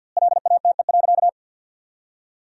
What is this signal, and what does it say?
Morse code, heard as a keyed tone, about 50 wpm, 700 Hz. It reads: LATE.
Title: 2x Word-Spacing - Sentences from Top 300 Words - 50wpm